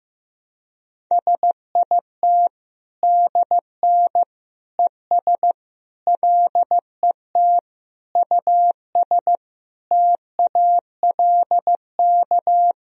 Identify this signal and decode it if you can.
Morse code, heard as a keyed tone, about 15 wpm, 700 Hz. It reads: SIT DN ES LET US TALK
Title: Sentences from Top 500 Words and Abbreviations 15wpm